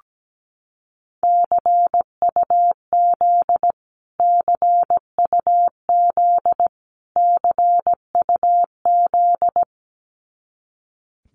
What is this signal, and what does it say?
Morse code, heard as a keyed tone, about 17 wpm, 700 Hz. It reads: CUZ CUZ CUZ
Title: Abbreviations Repeated 3x - 17wpm